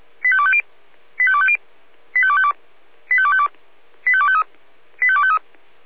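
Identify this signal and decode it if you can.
Five-tone selective call (ZVEI), heard as quick runs of five tones, tones of about 70 ms. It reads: 85250 85250 85252 85252 85253 85253
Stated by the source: ZVEI five-tone call received by SDR.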